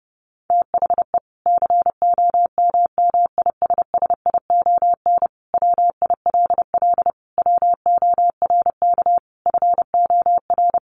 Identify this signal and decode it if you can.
Morse code, heard as a keyed tone, about 29 wpm, 700 Hz. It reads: THE COMMISSION WILL WORK FOR
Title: Sentences from Top 1400 Words 30wpm